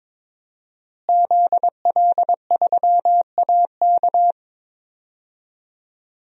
Morse code, heard as a keyed tone, about 22 wpm, 700 Hz. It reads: ZL3AK
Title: DXCC-100 Call Signs - 5 Characters - Repeated 2x - 22wpm